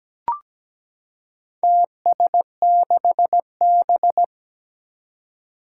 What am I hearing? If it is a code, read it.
Morse code, heard as a keyed tone, about 17 wpm, 700 Hz. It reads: TS6B
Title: Four Challenging Letter-Number Combinations 17wpm